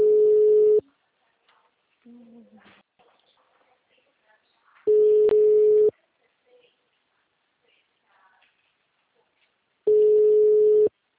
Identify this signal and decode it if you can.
Ringback tone (Europe, 425 Hz).
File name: out- -2055- - - .34662